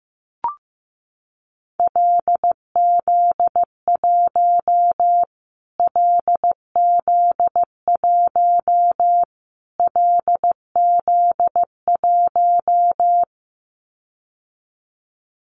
Morse code, heard as a keyed tone, about 15 wpm, 700 Hz. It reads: LZ1 LZ1 LZ1
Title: Callsign Prefixes for Top 25 DXCC - Repeated 3x - 15wpm